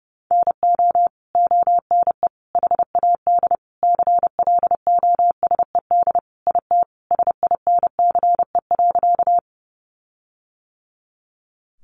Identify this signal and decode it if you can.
Morse code, heard as a keyed tone, about 30 wpm, 700 Hz. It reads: NO ONE HAD CLOSED IT SINCE.
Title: Sentences from Top 1400 Words 30wpm